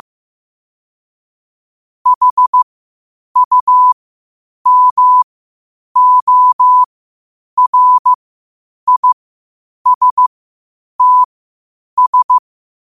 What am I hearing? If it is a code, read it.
Morse code, heard as a keyed tone, about 15 wpm, spaced as for 8 wpm, 1000 Hz. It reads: HUMORISTS
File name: qotdfiles / 8wpm